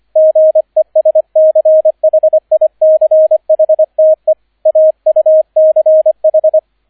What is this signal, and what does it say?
Morse code, heard as a keyed tone, about 25 wpm, 630 Hz, with radio noise. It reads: GESCHICHTE AUCH
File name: 003 Dritter Streich 25wpm